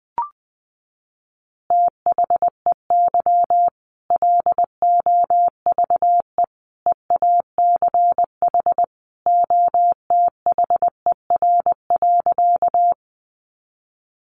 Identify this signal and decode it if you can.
Morse code, heard as a keyed tone, about 20 wpm, 700 Hz. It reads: THEY LOVE EACH OTHER.
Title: Easy Phrases 20wpm